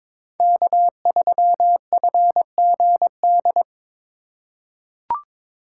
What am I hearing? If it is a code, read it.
Morse code, heard as a keyed tone, about 22 wpm, 700 Hz. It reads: K3FGD
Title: USA Call Signs - 5 Characters - 4x Character Spacing - 22wpm